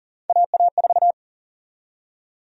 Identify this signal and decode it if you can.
Morse code, heard as a keyed tone, about 40 wpm, 700 Hz. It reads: AA4
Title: USA Call Sign Prefixes - 40wpm